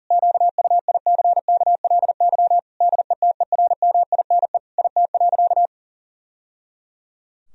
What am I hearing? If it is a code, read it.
Morse code, heard as a keyed tone, about 40 wpm, 700 Hz. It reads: QUICKLY DETERMINE IT.
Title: Sentences from Top 1000 Words - Review Part 4 - 40wpm